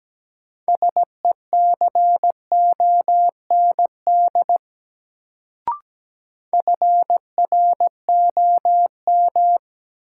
Morse code, heard as a keyed tone, about 17 wpm, 700 Hz. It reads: SECOND FROM
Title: Sets of 2 Words - New Word Drill - 17wpm